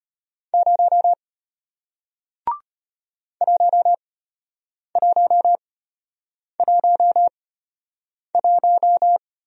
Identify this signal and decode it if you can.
Morse code, heard as a keyed tone, about 33 wpm, spaced as for 8 wpm, 700 Hz. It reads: 0 1111